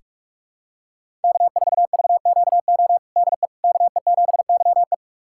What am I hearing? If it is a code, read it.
Morse code, heard as a keyed tone, about 46 wpm, 700 Hz. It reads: K4VQO DE KE7YE
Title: USA Callsign DE Callsign - 5 Characters - 45wpm